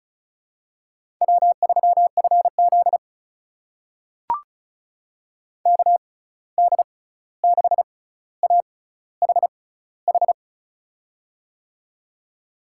Morse code, heard as a keyed tone, about 34 wpm, spaced as for 16 wpm, 700 Hz. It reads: W3FZ KD6AHH
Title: USA Call Signs - 6x Character Spacing - 35wpm